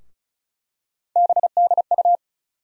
Morse code, heard as a keyed tone, about 34 wpm, 700 Hz. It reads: BDU
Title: USA Call Sign Postfixes - 3 Letters 35wpm